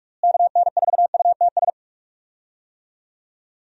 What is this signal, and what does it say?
Morse code, heard as a keyed tone, about 43 wpm, 700 Hz. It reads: KN4UTS
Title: USA Call Signs - 6 Characters - Repeated 3x - 45wpm